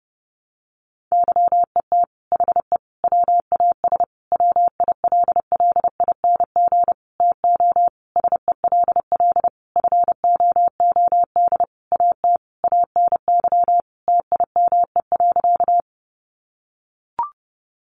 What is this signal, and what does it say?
Morse code, heard as a keyed tone, about 30 wpm, 700 Hz. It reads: YET HE WAS WILLING TO SELL FOOD AT ANY TIME.
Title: Sentences from Top 1100 Words 30wpm